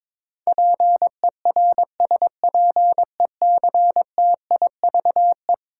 Morse code, heard as a keyed tone, about 21 wpm, 700 Hz. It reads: PERSPECTIVE
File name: practice / 22wpm / Eleven Letter Words - 22wpm